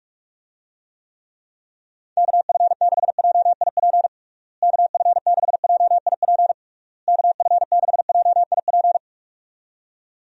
Morse code, heard as a keyed tone, about 46 wpm, 700 Hz. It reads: KF6JIP KF6JIP KF6JIP
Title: USA Call Signs - 6 Characters - Repeated 3x - 45wpm